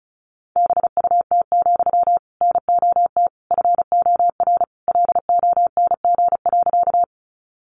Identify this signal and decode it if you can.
Morse code, heard as a keyed tone, about 34 wpm, 700 Hz. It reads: BUT, NOT FOR LONG.
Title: Sentences from Top 600 Words 35wpm